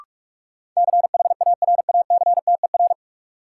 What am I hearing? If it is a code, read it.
Morse code, heard as a keyed tone, about 46 wpm, 700 Hz. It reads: CHARACTER
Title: Sentences from Top 1200 Words - New Word Drill Alt - 45wpm